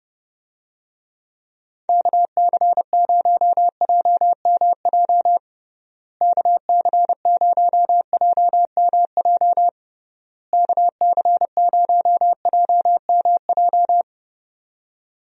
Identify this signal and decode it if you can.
Morse code, heard as a keyed tone, about 32 wpm, 700 Hz. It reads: KC0JMJ KC0JMJ KC0JMJ
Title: USA Call Signs - 6 Characters - Repeated 3x - 30wpm